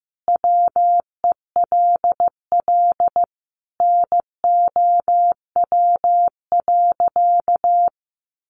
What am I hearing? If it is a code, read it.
Morse code, heard as a keyed tone, about 15 wpm, 700 Hz. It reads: WELL NOW.
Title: Sentences from Top 1000 Words 15wpm